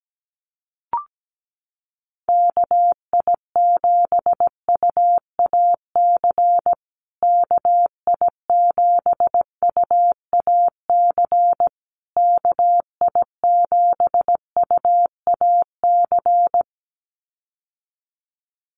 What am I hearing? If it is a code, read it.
Morse code, heard as a keyed tone, about 17 wpm, 700 Hz. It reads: KI7UAC KI7UAC KI7UAC
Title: USA Call Signs Repeated 3x 17wpm